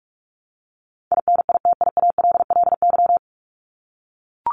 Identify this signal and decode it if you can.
Morse code, heard as a keyed tone, about 44 wpm, 700 Hz. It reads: INITIALLY
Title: Top 1251-1500 Words 45wpm